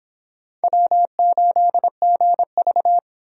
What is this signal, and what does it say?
Morse code, heard as a keyed tone, about 26 wpm, 700 Hz. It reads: W8GV